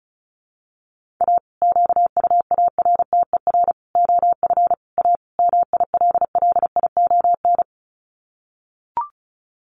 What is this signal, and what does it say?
Morse code, heard as a keyed tone, about 34 wpm, 700 Hz. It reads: A QUARTER OF A MILLION
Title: Sets of 5 Words Encore2 35wpm